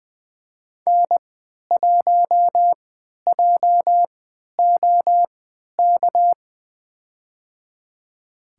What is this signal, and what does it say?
Morse code, heard as a keyed tone, about 20 wpm, spaced as for 11 wpm, 700 Hz. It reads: N1JOK
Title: USA Call Signs - 5 Characters - 3x Character Spacing - 20wpm